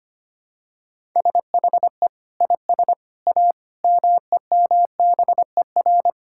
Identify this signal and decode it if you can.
Morse code, heard as a keyed tone, about 25 wpm, 700 Hz. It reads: SHE IS A MEMBER
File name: practice / 25wpm / Sentences with 3-Word Phrases - 25wpm